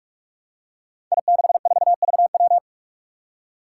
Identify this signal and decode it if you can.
Morse code, heard as a keyed tone, about 46 wpm, 700 Hz. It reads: I64VW
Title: Five Challenging Letter-Number Combinations 45wpm